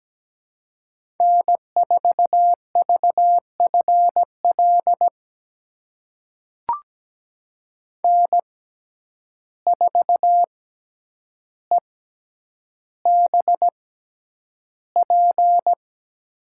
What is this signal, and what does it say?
Morse code, heard as a keyed tone, about 17 wpm, spaced as for 9 wpm, 700 Hz. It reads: N4VFL N4EBP
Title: USA Call Signs - 5 Characters - 6x Character Spacing - 17wpm